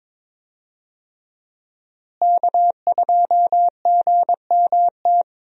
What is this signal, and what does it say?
Morse code, heard as a keyed tone, about 22 wpm, 700 Hz. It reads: K2GMT
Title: USA Call Signs - 5 Characters - Repeated 3x - 22wpm